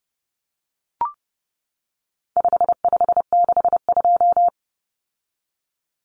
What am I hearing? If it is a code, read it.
Morse code, heard as a keyed tone, about 30 wpm, 700 Hz. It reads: 5562
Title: Common Cut Numbers - 4-Digit 30wpm